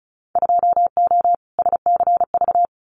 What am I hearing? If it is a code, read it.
Morse code, heard as a keyed tone, about 36 wpm, 700 Hz. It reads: 2O SCV
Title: Winter Field Day Exchanges Encore 35wpm